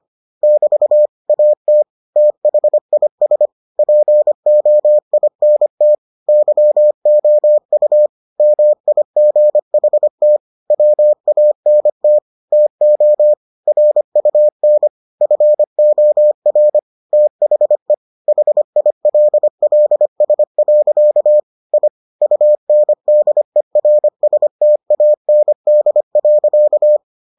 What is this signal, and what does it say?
Morse code, heard as a keyed tone, about 25 wpm, 600 Hz. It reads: <BT> AT THIS POINT YOU MIGHT WANT TO RUN FOR THE HILLS. I UNDERSTAND.